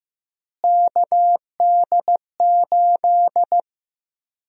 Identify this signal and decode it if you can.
Morse code, heard as a keyed tone, about 16 wpm, 700 Hz. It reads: KD8
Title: USA Call Sign Prefixes - Repeated 2x - 15wpm